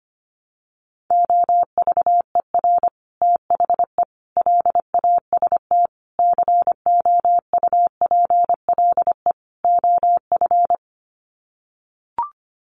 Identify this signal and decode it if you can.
Morse code, heard as a keyed tone, about 25 wpm, 690 Hz. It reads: OVER THE LAST COUPLE OF
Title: Sets of 5 Words Encore 25wpm